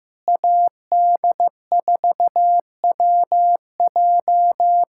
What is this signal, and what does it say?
Morse code, heard as a keyed tone, about 15 wpm, 700 Hz. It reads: AD4WJ
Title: USA Callsign DE Callsign 15wpm